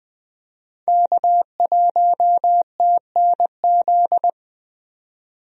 Morse code, heard as a keyed tone, about 20 wpm, 700 Hz. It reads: K1TNZ
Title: USA Call Signs - 5 Characters - 20wpm